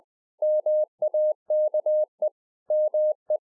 Morse code, heard as a keyed tone, about 20 wpm, 610 Hz. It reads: MAKE ME